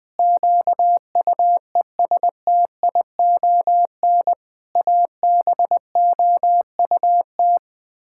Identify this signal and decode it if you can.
Morse code, heard as a keyed tone, about 20 wpm, 700 Hz. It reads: QUESTION ABOUT
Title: Sets of 6 Words - 20wpm